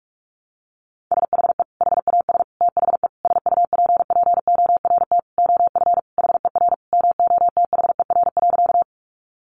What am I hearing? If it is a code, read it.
Morse code, heard as a keyed tone, about 44 wpm, 700 Hz. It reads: SHE HAS THE SUPPORT OF HER MOTHER.
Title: Sentences from Top 800 Words 45wpm